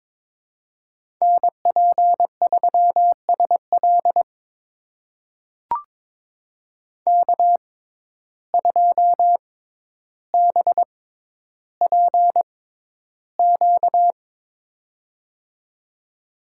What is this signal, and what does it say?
Morse code, heard as a keyed tone, about 23 wpm, spaced as for 11 wpm, 700 Hz. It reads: NP3SL K2BPQ
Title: USA Call Signs - 5 Characters - 6x Character Spacing - 22wpm